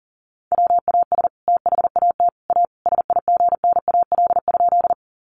Morse code, heard as a keyed tone, about 40 wpm, 700 Hz. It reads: WAS THAT A SIGNAL?
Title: Sentences from Top 1400 Words 40wpm